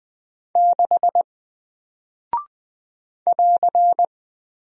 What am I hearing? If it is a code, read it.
Morse code, heard as a keyed tone, about 21 wpm, 700 Hz. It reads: 6 <AR>
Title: Single Character (Letter:Number:Punctuation:Prosign) Alt 20wpm